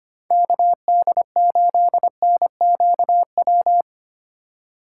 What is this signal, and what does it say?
Morse code, heard as a keyed tone, about 25 wpm, 700 Hz. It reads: KD8NQW
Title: USA Call Signs - 3x Character Spacing - 25wpm